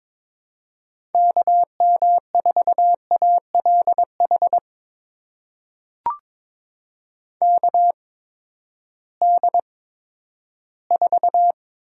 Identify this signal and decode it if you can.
Morse code, heard as a keyed tone, about 22 wpm, spaced as for 12 wpm, 700 Hz. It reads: KM4ALH KD4
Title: USA Call Signs - 6 Characters - 8x Character Spacing - 22wpm